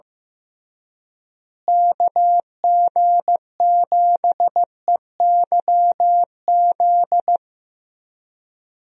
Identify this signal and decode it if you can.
Morse code, heard as a keyed tone, about 15 wpm, 700 Hz. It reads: KG7EYZ
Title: USA Call Signs - 6 Characters - Repeated 3x - 15wpm